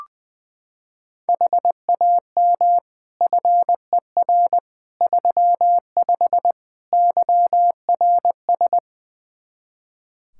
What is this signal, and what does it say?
Morse code, heard as a keyed tone, about 20 wpm, 700 Hz. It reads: HAM FER 35 YRS
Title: K1USN SST QSO Practice 20wpm